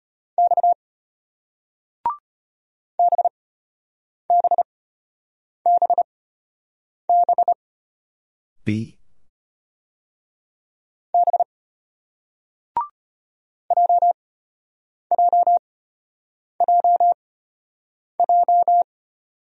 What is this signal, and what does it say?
Morse code, heard as a keyed tone, about 35 wpm, spaced as for 8 wpm, 700 Hz. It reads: X BBBB B JJJJ